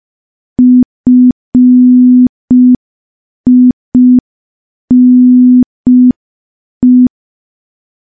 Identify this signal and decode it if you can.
Morse code, heard as a keyed tone, about 5 wpm, 250 Hz. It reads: FINE
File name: morse / morse-training-05WPM